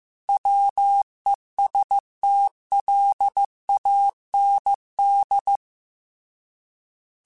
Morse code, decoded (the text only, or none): WESTLAND